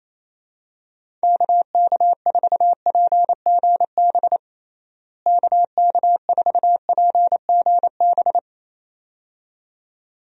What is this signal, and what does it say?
Morse code, heard as a keyed tone, about 28 wpm, 700 Hz. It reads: KK4PGB KK4PGB
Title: USA Call Signs - 6 Characters - Repeated 2x - 28wpm